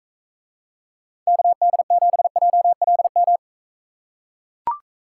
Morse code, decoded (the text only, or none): KD7JLM